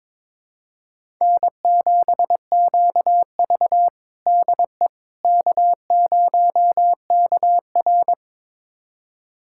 Morse code, heard as a keyed tone, about 22 wpm, 700 Hz. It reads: N7QV DE K0KR
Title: USA Callsign DE Callsign - 4 Characters - 22wpm